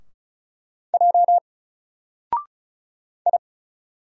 Morse code, heard as a keyed tone, about 35 wpm, 700 Hz. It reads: J I
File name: Single-Letter-Number-35wpm